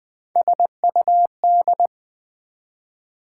Morse code, heard as a keyed tone, about 20 wpm, 700 Hz. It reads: SUD
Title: Build A Word - 3 Letters At A Time - 8-Letter Words - 20wpm